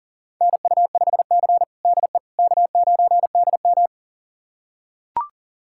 Morse code, heard as a keyed tone, about 40 wpm, 690 Hz. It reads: NU5C DE K9DM
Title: USA Callsign DE Callsign - 4 Characters - 40wpm